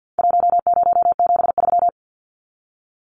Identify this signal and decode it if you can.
Morse code, heard as a keyed tone, about 50 wpm, 700 Hz. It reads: 1073
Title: Numbers - 4-Digit 50wpm